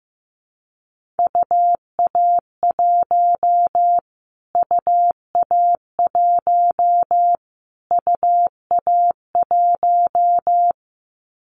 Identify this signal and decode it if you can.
Morse code, heard as a keyed tone, about 15 wpm, 700 Hz. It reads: UA1 UA1 UA1
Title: Callsign Prefixes for Top 25 DXCC - Repeated 3x - 15wpm